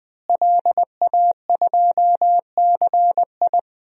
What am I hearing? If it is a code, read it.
Morse code, heard as a keyed tone, about 20 wpm, 700 Hz. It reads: LA2CI
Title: DXCC-100 Call Signs - 5 Characters - Repeated 2x - 20wpm